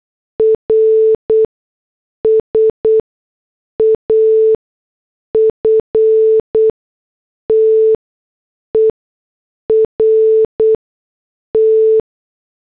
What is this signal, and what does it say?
Morse code, heard as a keyed tone, about 8 wpm, spaced as for 6 wpm, 440 Hz. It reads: RS AFTER T
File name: morse / headlines